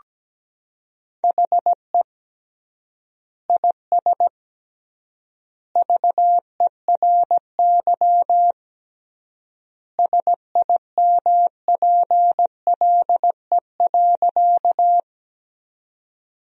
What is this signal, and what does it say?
Morse code, heard as a keyed tone, about 17 wpm, 700 Hz. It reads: HE IS VERY SIMPLE.
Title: 3x Word-Spacing - Sentences from Top 400 Words - 17wpm